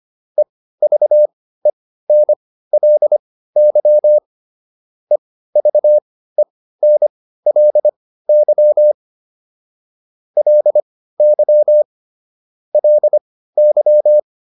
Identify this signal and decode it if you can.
Morse code, heard as a keyed tone, about 25 wpm, spaced as for 15 wpm, 600 Hz. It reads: EVENLY EVENLY LY LY